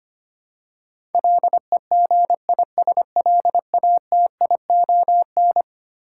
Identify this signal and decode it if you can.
Morse code, heard as a keyed tone, about 25 wpm, 700 Hz. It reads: LEGISLATION
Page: morse-code-ninja-practice-25wpm